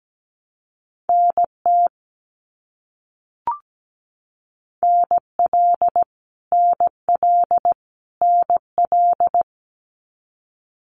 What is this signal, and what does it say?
Morse code, heard as a keyed tone, about 17 wpm, 700 Hz. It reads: NT NL NL NL
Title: Canadian Provinces and Territories - Repeated 3x - 17wpm